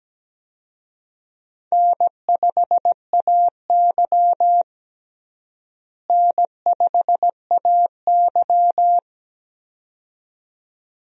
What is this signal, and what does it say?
Morse code, heard as a keyed tone, about 17 wpm, 700 Hz. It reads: N5AY N5AY
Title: USA Call Signs - 4 Characters - Repeated 2x - 17wpm